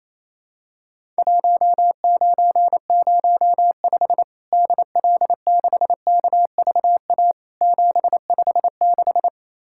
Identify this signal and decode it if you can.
Morse code, heard as a keyed tone, about 27 wpm, 700 Hz. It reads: 1905 DL6KVA 756
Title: Worked All Europe DX Contest - QTC Exchange - 28wpm